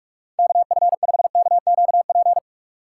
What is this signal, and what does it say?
Morse code, heard as a keyed tone, about 42 wpm, 700 Hz. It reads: KF5KQP
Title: USA Call Signs - 2x Character Spacing - 45wpm